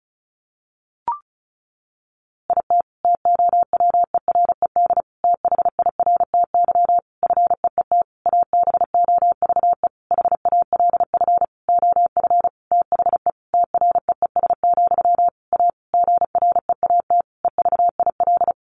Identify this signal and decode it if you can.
Morse code, heard as a keyed tone, about 34 wpm, 700 Hz. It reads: IT TOWERED THIRTY FEET ABOVE HALF OF THE TREES, A GREAT EVIL
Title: A Sound Of Thunder 35wpm